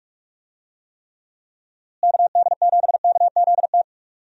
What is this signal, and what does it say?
Morse code, heard as a keyed tone, about 45 wpm, 700 Hz. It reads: KD7KZT
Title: USA Call Signs - 6 Characters - Repeated 2x - 45wpm